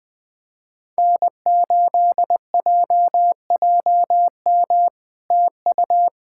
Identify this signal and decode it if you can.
Morse code, heard as a keyed tone, about 20 wpm, 700 Hz. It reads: N8JJM TU
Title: POTA QSO Practice - 20wpm